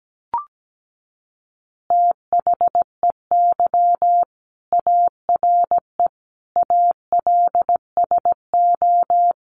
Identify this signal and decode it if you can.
Morse code, heard as a keyed tone, about 17 wpm, 700 Hz. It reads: THEY ARE ALSO
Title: Additional Easy Phrases 17wpm